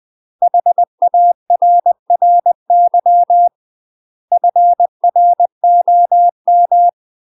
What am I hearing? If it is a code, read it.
Morse code, heard as a keyed tone, about 20 wpm, 700 Hz. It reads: HARRY FROM
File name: UK MASTER QSOs TRAINING / 20WPM / UK MASTER QSOs 0072